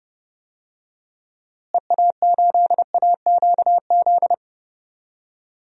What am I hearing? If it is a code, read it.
Morse code, heard as a keyed tone, about 32 wpm, 700 Hz. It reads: EA8AQZ
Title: DXCC-100 Call Signs - 6 Characters - Repeated 2x - 30wpm